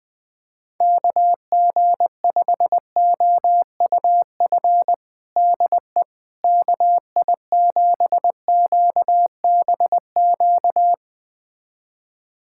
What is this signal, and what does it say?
Morse code, heard as a keyed tone, about 21 wpm, 700 Hz. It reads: KG5OUF DE KI7QBQ
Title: USA Callsign DE Callsign 20wpm